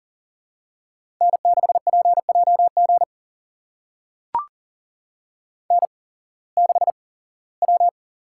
Morse code, heard as a keyed tone, about 39 wpm, spaced as for 21 wpm, 700 Hz. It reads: N6PJG N6W
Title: USA Call Signs - 5 Characters - 8x Character Spacing - 40wpm